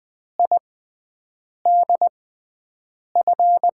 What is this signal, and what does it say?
Morse code, heard as a keyed tone, about 20 wpm, spaced as for 7 wpm, 700 Hz. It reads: IDF
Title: USA Call Signs - 6x Character Spacing - 20wpm